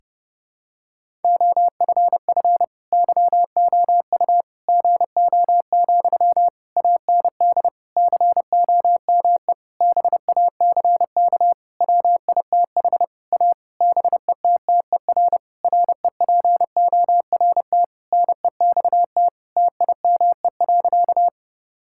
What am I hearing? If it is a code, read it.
Morse code, heard as a keyed tone, about 32 wpm, 700 Hz. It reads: OFF YOU GO, AND COME BACK WITH A BETTER REPORT NEXT TIME.
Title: Sentences from Top 1000 Words - Review Part 6 - 30wpm